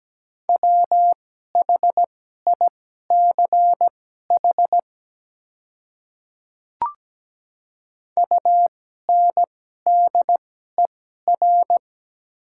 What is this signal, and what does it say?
Morse code, heard as a keyed tone, about 17 wpm, spaced as for 12 wpm, 700 Hz. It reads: WHICH UNDER
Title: 2x Character Spacing - Top 300 Words - 17wpm